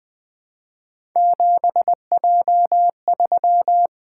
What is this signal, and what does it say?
Morse code, heard as a keyed tone, about 20 wpm, 700 Hz. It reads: 7J3
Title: Three Letter-Number Combinations - 20wpm